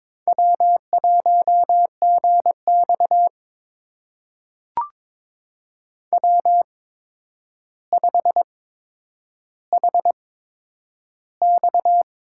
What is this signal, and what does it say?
Morse code, heard as a keyed tone, about 22 wpm, spaced as for 9 wpm, 700 Hz. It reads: W1GX W5HX
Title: USA Call Signs - 8x Character Spacing - 22wpm